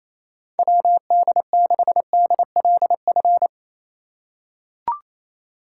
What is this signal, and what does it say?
Morse code, heard as a keyed tone, about 27 wpm, 700 Hz. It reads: WD6DLF
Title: USA Call Signs - 6 Characters - 6x Character Spacing - 28wpm